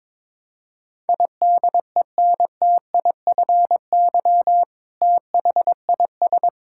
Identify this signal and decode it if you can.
Morse code, heard as a keyed tone, about 21 wpm, 700 Hz. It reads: IDENTIFY THIS
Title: Sentences from Top 1000 Words - Review Part 6 - 22wpm